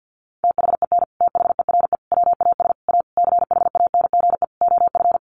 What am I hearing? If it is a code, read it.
Morse code, heard as a keyed tone, about 48 wpm, 700 Hz. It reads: THEN THERE WAS A CHANGE OF
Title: Sentences from Top 1100 Words 50wpm